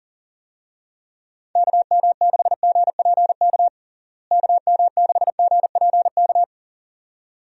Morse code, heard as a keyed tone, about 40 wpm, 700 Hz. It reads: KM6GPK KM6GPK
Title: USA Call Signs - 6 Characters - Repeated 2x - 40wpm